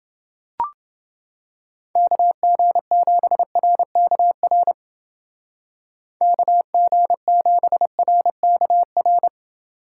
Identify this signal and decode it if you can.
Morse code, heard as a keyed tone, about 30 wpm, 700 Hz. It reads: KG7RKR KG7RKR